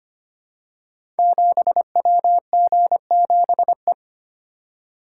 Morse code, heard as a keyed tone, about 25 wpm, 700 Hz. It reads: 7WG7E
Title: Code Groups - 5 Letter:Numbers - 25wpm